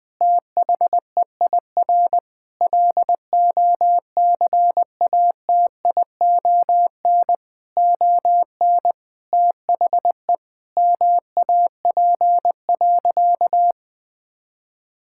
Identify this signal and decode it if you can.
Morse code, heard as a keyed tone, about 21 wpm, 700 Hz. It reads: THEIR LOCATION ON THE MAP.
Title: Sentences from Top 1000 Words - Review Part 6 - 20wpm